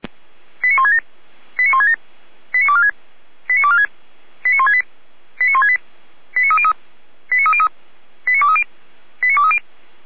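Five-tone selective call (ZVEI), heard as quick runs of five tones, tones of about 70 ms. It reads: 89167 89167 89246 89246 89168 89168 89303 89303 89230 89230
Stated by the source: ZVEI five-tone selective call 89208 received via SDR.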